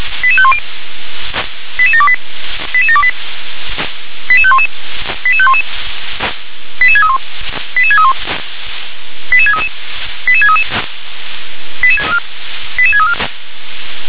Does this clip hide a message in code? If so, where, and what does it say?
Yes: five-tone selective call (ZVEI) from 0.2 to 13.1 s, heard as quick runs of five tones, tones of about 70 ms. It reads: 88510 88728 88728 88511 88511 88631 88631 88633 88633 88634